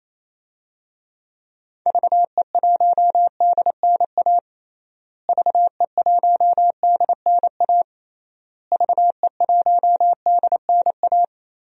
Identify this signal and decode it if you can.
Morse code, heard as a keyed tone, about 27 wpm, 700 Hz. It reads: VE1DNA VE1DNA VE1DNA
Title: DXCC-100 Call Signs - 6 Characters - Repeated 3x - 28wpm